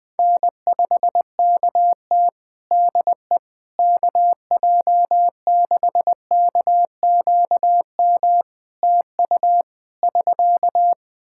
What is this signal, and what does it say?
Morse code, heard as a keyed tone, about 20 wpm, 700 Hz. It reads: N5KT DE KJ6KQM TU <SK>
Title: Practice QSO - Exchange 4 - Ending - 20wpm